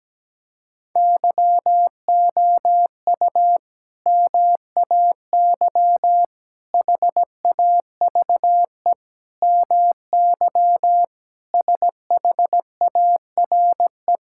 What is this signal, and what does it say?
Morse code, heard as a keyed tone, about 17 wpm, 700 Hz. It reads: YOU MAY HAVE MY SHARE
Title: Sentences from Top 1000 Words - Review Part 6 - 17wpm